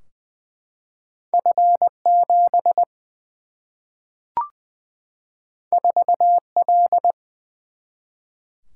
Morse code, heard as a keyed tone, about 20 wpm, 700 Hz. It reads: F 7 4 L
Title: Single-Letter Single-Number 20wpm